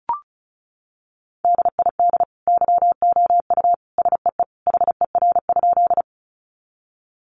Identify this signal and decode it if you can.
Morse code, heard as a keyed tone, about 34 wpm, 700 Hz. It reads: DID YOU SEE HER?
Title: Sentences from Top 100 Words 35wpm